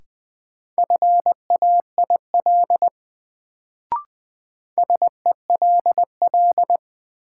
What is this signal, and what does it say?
Morse code, heard as a keyed tone, about 21 wpm, 700 Hz. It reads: FAIL SELL
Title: Sentences from Top 1100 Words - New Word Drill Alt - 20wpm